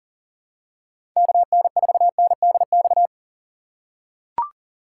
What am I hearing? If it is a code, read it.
Morse code, heard as a keyed tone, about 40 wpm, 700 Hz. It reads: KN4NDX
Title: USA Call Signs - 6 Characters - 4x Character Spacing - 40wpm